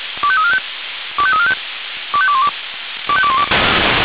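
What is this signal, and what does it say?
Five-tone selective call (ZVEI), heard as quick runs of five tones, tones of about 70 ms. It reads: 25345 25345 25212 25212
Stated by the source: ZVEI five-tone call received by SDR.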